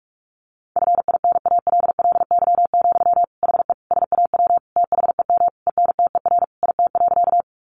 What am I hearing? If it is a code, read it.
Morse code, heard as a keyed tone, about 44 wpm, 700 Hz. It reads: FINALLY, HE SAW THEM ENTER IT.
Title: Sentences from Top 1300 Words 45wpm